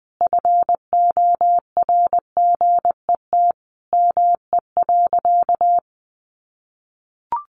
Morse code, heard as a keyed tone, about 20 wpm, 700 Hz. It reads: FORGET ME.
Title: Sentences from Top 1200 Words 20wpm